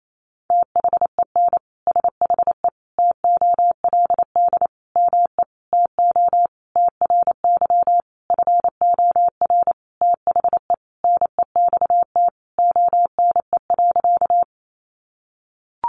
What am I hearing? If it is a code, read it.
Morse code, heard as a keyed tone, about 28 wpm, 700 Hz. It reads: THEN SHE TOLD ME TO TRY FOR THE NEXT ONE.
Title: Sentences from Top 400 Words 28wpm